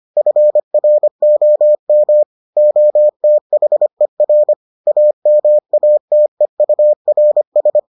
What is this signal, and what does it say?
Morse code, heard as a keyed tone, about 25 wpm, 600 Hz. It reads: FROM OTHER AMATEURS